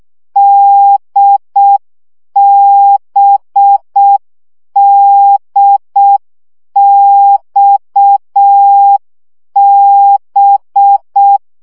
Morse code, heard as a keyed tone, about 6 wpm, 800 Hz. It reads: DBDXB
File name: CW gyakorlas d b x 5 baud